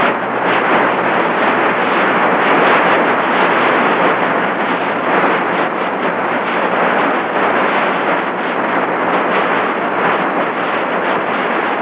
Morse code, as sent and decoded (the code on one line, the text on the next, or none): none